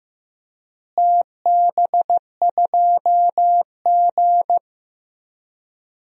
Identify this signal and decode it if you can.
Morse code, heard as a keyed tone, about 16 wpm, 700 Hz. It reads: TB2G
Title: DXCC-100 Call Signs - 4 Characters - Repeated 3x - 15wpm